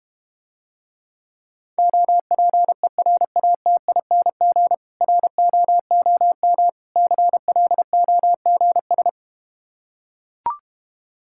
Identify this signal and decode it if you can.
Morse code, heard as a keyed tone, about 32 wpm, 700 Hz. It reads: OPERATING ROOM CLOGS